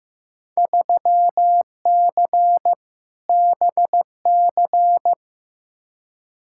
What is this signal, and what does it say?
Morse code, heard as a keyed tone, about 16 wpm, 700 Hz. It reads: 3C BC
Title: Field Day Exchanges Encore - 15wpm